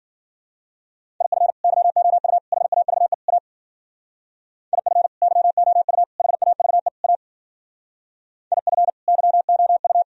Morse code, heard as a keyed tone, about 52 wpm, 700 Hz. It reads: IF YOU HAVE A IF YOU HAVE A IF YOU